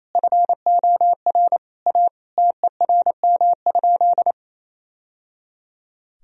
Morse code, heard as a keyed tone, about 28 wpm, 700 Hz. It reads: FOR A TERM?
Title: Sentences from Top 1000 Words - Review Part 4 - 28wpm